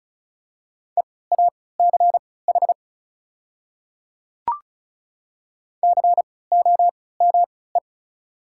Morse code, heard as a keyed tone, about 34 wpm, spaced as for 20 wpm, 690 Hz. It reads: EACH COME
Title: 3x Character Spacing - Top 100 Words - 35wpm